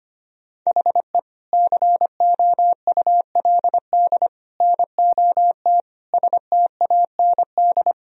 Morse code, heard as a keyed tone, about 25 wpm, 700 Hz. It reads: HE COULD NOT STAND
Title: Sentences from Top 1000 Words - Review Part 1 - 25wpm